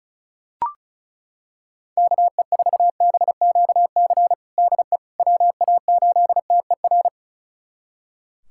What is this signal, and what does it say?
Morse code, heard as a keyed tone, about 34 wpm, 700 Hz. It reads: KE4BQC DE WA8TER
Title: USA Callsign DE Callsign 35wpm